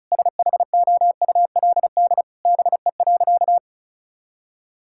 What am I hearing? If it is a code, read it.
Morse code, heard as a keyed tone, about 34 wpm, 700 Hz. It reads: SHOULD BE.
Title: Sentences from Top 1000 Words - Review Part 5 - 35wpm